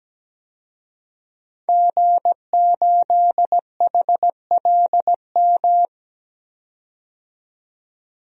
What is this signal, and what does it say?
Morse code, heard as a keyed tone, about 17 wpm, 700 Hz. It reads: G8HLM
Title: DXCC-100 Call Signs - 5 Characters - 17wpm